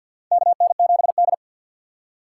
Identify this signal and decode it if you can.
Morse code, heard as a keyed tone, about 52 wpm, 700 Hz. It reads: KN7D
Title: K1USN SST QSO Practice - 50wpm